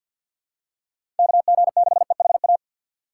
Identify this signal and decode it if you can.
Morse code, heard as a keyed tone, about 53 wpm, 700 Hz. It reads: KG6EHA
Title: USA Call Signs - 6 Characters - 4x Character Spacing - 50wpm